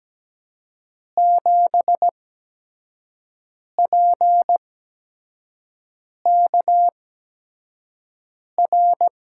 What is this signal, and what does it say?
Morse code, heard as a keyed tone, about 17 wpm, spaced as for 5 wpm, 700 Hz. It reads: 7PKR
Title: USA Call Signs - 8x Character Spacing - 17wpm